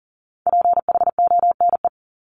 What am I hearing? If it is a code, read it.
Morse code, heard as a keyed tone, about 38 wpm, 700 Hz. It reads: PHONE
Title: Multi-letter Phonemes 40wpm